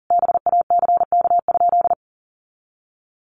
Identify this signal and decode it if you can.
Morse code, heard as a keyed tone, about 40 wpm, 700 Hz. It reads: BACK?
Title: Sentences from Top 400 Words 40wpm